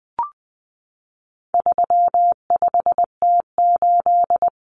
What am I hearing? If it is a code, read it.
Morse code, heard as a keyed tone, about 20 wpm, 700 Hz. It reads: 35T8
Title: Common Cut Numbers - 4-Digit 20wpm